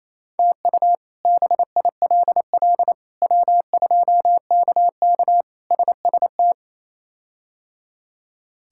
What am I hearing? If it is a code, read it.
Morse code, heard as a keyed tone, about 27 wpm, 700 Hz. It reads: TU BILL W2KK SST
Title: K1USN SST QSO Practice - 28wpm